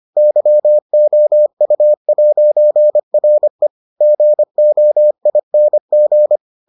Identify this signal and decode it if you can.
Morse code, heard as a keyed tone, about 25 wpm, 600 Hz. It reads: YOU'RE GOING